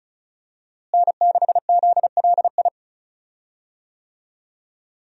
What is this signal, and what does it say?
Morse code, heard as a keyed tone, about 36 wpm, 700 Hz. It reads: N6ZLI
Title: USA Call Signs - 5 Characters - Repeated 2x - 35wpm